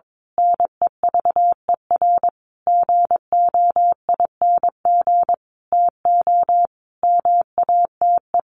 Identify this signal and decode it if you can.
Morse code, heard as a keyed tone, about 22 wpm, 700 Hz. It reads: NEVER GOING TO MATE
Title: A Sound Of Thunder 22wpm